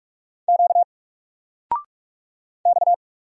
Morse code, heard as a keyed tone, about 46 wpm, 700 Hz. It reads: Q X
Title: Single Letter-Number 45wpm